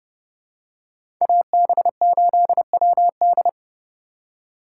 Morse code, heard as a keyed tone, about 30 wpm, 700 Hz. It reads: AB8WD
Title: USA Call Signs - 4x Character Spacing - 30wpm